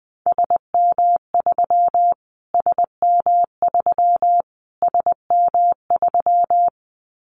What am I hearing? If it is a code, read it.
Morse code, heard as a keyed tone, about 20 wpm, 700 Hz. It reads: SM3 SM3 SM3
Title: Callsign Prefixes for Top 25 DXCC - Repeated 3x - 20wpm